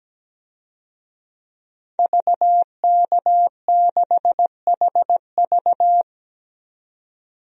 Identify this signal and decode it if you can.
Morse code, heard as a keyed tone, about 17 wpm, 700 Hz. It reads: VK6HV
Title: DXCC-100 Call Signs - 5 Characters - 17wpm